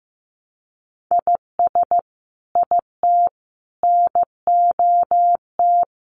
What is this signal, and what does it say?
Morse code, heard as a keyed tone, about 15 wpm, 700 Hz. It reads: IS IT NOT
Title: Sentences from Top 1400 Words 15wpm